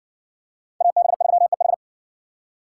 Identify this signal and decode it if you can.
Morse code, heard as a keyed tone, about 60 wpm, 700 Hz. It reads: AB3EH